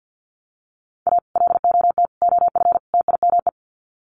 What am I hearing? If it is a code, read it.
Morse code, heard as a keyed tone, about 50 wpm, 700 Hz. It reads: A LOT OF TIME
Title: Sets of 4 Words 50wpm